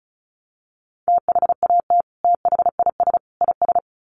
Morse code, heard as a keyed tone, about 34 wpm, 700 Hz. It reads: THAT THIS IS
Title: Sets of 3 Words 35wpm